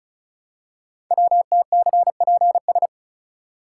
Morse code, heard as a keyed tone, about 36 wpm, 700 Hz. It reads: WTCPS
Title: Code Groups - 5 Letters - 35wpm